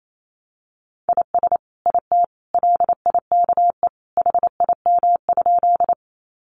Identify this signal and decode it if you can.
Morse code, heard as a keyed tone, about 28 wpm, 700 Hz. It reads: IS IT LIKE HIM?
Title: Sentences from Top 100 Words 28wpm